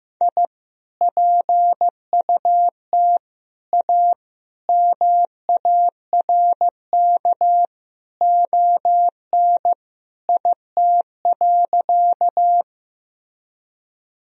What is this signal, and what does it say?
Morse code, heard as a keyed tone, about 15 wpm, 700 Hz. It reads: I PUT A MARK ON IT.
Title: Sentences from Top 1000 Words - Review Part 1 - 15wpm